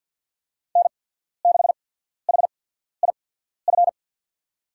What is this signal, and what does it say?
Morse code, heard as a keyed tone, about 51 wpm, spaced as for 14 wpm, 700 Hz. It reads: N6HIF
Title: USA Call Signs - 5 Characters - 8x Character Spacing - 50wpm